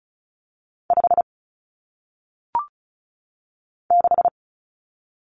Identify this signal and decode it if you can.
Morse code, heard as a keyed tone, about 36 wpm, 700 Hz. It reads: 5 6
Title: Numbers - 1-Digit 35wpm